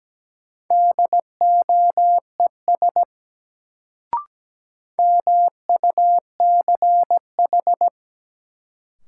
Morse code, heard as a keyed tone, about 17 wpm, 700 Hz. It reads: DOES MUCH
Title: Sentences from Top 200 Words - New Word Drill - 17wpm